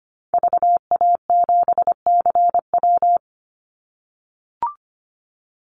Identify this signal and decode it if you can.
Morse code, heard as a keyed tone, about 25 wpm, 700 Hz. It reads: VA7CW
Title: Call Signs - 25wpm